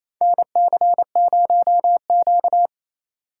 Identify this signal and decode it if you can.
Morse code, heard as a keyed tone, about 28 wpm, 700 Hz. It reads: NC0Q
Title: USA Call Signs - 4x Character Spacing - 28wpm